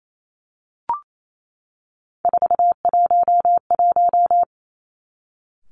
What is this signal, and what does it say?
Morse code, heard as a keyed tone, about 28 wpm, 700 Hz. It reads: 411
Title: Three Challenging Letter-Number Combinations 28wpm